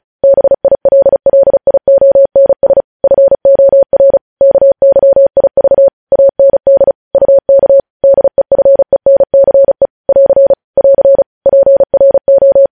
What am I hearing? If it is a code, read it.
Morse code, heard as a keyed tone, about 36 wpm, 550 Hz. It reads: BILLIONS FOR KYIV AND UK DEFENCE <AR> <AR> PRO
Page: News Headlines in Morse Code at 35 WPM